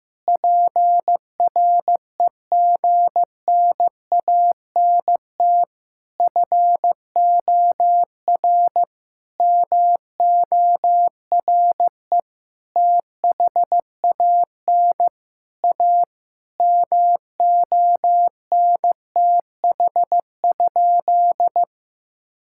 Morse code, heard as a keyed tone, about 15 wpm, 700 Hz. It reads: PREGNANT FOR MORE THAN A MONTH?
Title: Sentences from Top 1000 Words - Review Part 6 - 15wpm